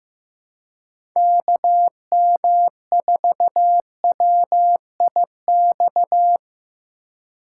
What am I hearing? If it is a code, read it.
Morse code, heard as a keyed tone, about 15 wpm, 700 Hz. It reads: KM4WIX
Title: USA Call Signs - 6 Characters - 2x Character Spacing - 15wpm